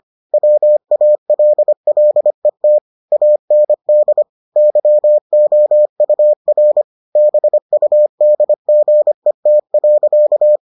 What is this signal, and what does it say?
Morse code, heard as a keyed tone, about 25 wpm, 600 Hz. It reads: WALLET AND YOUR BUDGET.